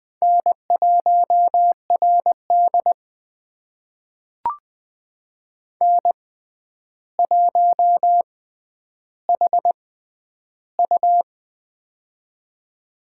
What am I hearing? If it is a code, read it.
Morse code, heard as a keyed tone, about 20 wpm, spaced as for 10 wpm, 700 Hz. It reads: N1RD N1HU
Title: USA Call Signs - 6x Character Spacing - 20wpm